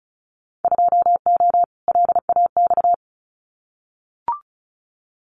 Morse code, heard as a keyed tone, about 34 wpm, 700 Hz. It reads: 2O LAX
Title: Winter Field Day Exchanges 35wpm